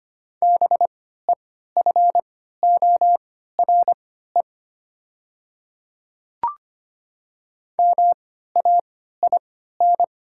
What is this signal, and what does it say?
Morse code, heard as a keyed tone, about 25 wpm, spaced as for 14 wpm, 700 Hz. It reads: BEFORE MAIN
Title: 3x Character Spacing - Top 300 Words - 25wpm